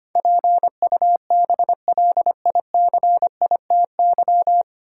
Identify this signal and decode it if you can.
Morse code, heard as a keyed tone, about 25 wpm, 700 Hz. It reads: PUBLICITY
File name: Nine Letter Words - 25wpm